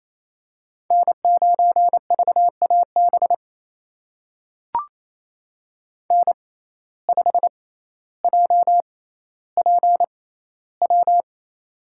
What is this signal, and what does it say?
Morse code, heard as a keyed tone, about 28 wpm, spaced as for 14 wpm, 700 Hz. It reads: N9VAB N5JPW
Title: USA Call Signs - 5 Characters - 6x Character Spacing - 28wpm